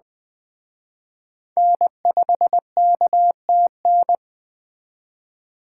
Morse code, heard as a keyed tone, about 20 wpm, 700 Hz. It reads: N5KTN